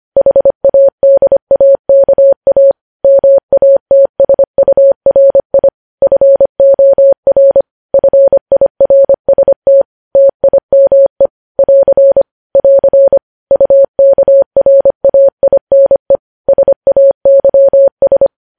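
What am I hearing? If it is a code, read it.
Morse code, heard as a keyed tone, about 25 wpm, 550 Hz. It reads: HADAKA MATSURI FOR FIRST TIME <AR> <AR> UKRAINE SAYS
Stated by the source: Morse code news headlines sent at 25 wpm.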